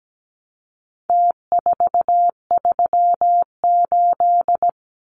Morse code, heard as a keyed tone, about 17 wpm, 700 Hz. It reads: T438
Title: Numbers - Contest Style - 4-Digit 17wpm